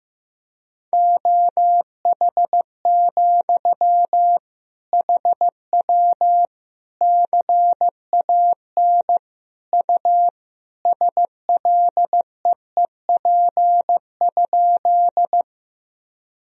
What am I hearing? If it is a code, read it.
Morse code, heard as a keyed tone, about 16 wpm, 700 Hz. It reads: OH, HW CAN U SLEEP?
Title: Sentences from Top 500 Words and Abbreviations 15wpm